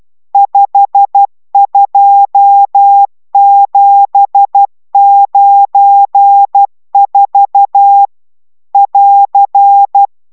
Morse code, heard as a keyed tone, about 12 wpm, 800 Hz. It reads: 52794 <AR>